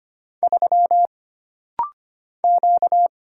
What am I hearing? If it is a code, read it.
Morse code, heard as a keyed tone, about 25 wpm, 700 Hz. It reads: 3 Q
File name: Single Character (Letter:Number:Punctuation:Prosign) - Rapid-Fire - 25wpm